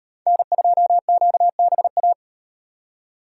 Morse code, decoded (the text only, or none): N2QBA